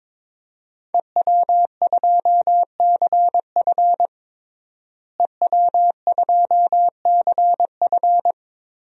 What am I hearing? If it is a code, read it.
Morse code, heard as a keyed tone, about 22 wpm, 700 Hz. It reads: EW2CF EW2CF
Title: DXCC-100 Call Signs - 5 Characters - Repeated 3x - 22wpm